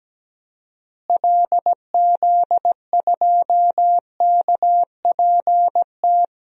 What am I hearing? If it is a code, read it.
Morse code, heard as a keyed tone, about 17 wpm, 700 Hz. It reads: LZ2KPT
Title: DXCC-100 Call Signs - 6 Characters - Repeated 3x - 17wpm